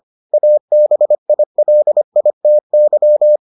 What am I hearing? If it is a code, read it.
Morse code, heard as a keyed tone, about 25 wpm, 600 Hz. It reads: ABILITY